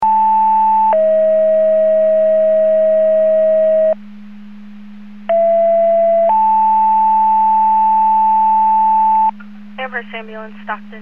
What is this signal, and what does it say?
Two-tone page, heard as a one-second tone, then a three-second tone, about 880 then 640 Hz, about 670 then 880 Hz, with speech. It is heard from 0.0 to 9.3 s.